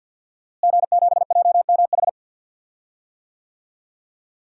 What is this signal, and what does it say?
Morse code, heard as a keyed tone, about 51 wpm, 700 Hz. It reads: G7JMH
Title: DXCC-100 Call Signs - 5 Characters - Repeated 3x - 50wpm